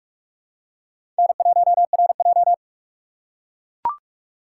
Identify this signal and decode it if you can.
Morse code, heard as a keyed tone, about 44 wpm, 700 Hz. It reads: N1RJ
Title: USA Call Signs - 4x Character Spacing - 45wpm